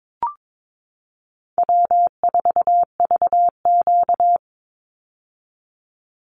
Morse code, heard as a keyed tone, about 22 wpm, 700 Hz. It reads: W4VQ
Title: Easy Call Signs - 22wpm